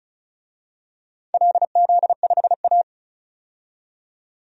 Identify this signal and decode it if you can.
Morse code, heard as a keyed tone, about 36 wpm, 700 Hz. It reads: LZ5A
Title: DXCC-100 Call Signs - 4 Characters - 35wpm